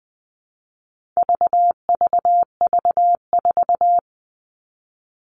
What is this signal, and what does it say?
Morse code, heard as a keyed tone, about 20 wpm, 700 Hz. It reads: VVV4
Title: All Cut Numbers - 4-Digit 20wpm